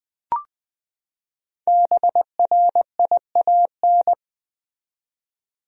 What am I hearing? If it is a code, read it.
Morse code, heard as a keyed tone, about 20 wpm, 700 Hz. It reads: BRIAN
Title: USA Common Names - Max 5 Letters - 20wpm